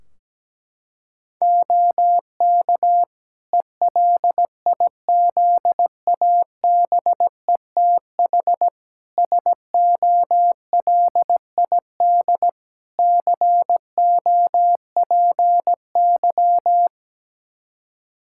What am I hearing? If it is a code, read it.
Morse code, heard as a keyed tone, about 17 wpm, 700 Hz. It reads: OK ELIZABETH SOLID COPY
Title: Practice QSO - Exchange 3 - More Info - 17wpm